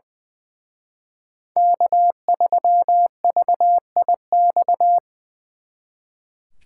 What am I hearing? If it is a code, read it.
Morse code, heard as a keyed tone, about 23 wpm, 700 Hz. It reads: K3VIX
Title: Speed Racing - USA 1x3 Call Signs 20wpm